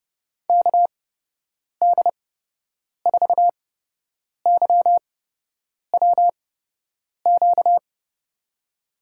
Morse code, heard as a keyed tone, about 30 wpm, spaced as for 8 wpm, 700 Hz. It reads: KD4YWQ
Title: USA Call Signs - 6 Characters - 8x Character Spacing - 30wpm